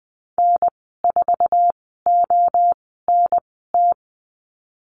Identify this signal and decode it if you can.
Morse code, heard as a keyed tone, about 19 wpm, spaced as for 14 wpm, 700 Hz. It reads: N4ONT
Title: 055 Lesson 10g - Letter r - Call Signs 20wpm